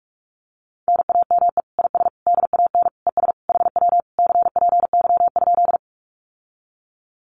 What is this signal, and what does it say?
Morse code, heard as a keyed tone, about 44 wpm, 700 Hz. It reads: NAME IS DAN ES HW CPY?
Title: Common QSO Phrases Encore 45wpm